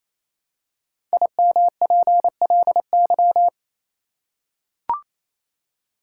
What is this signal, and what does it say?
Morse code, heard as a keyed tone, about 28 wpm, 700 Hz. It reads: IMPLY
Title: Five Letter Words - 28wpm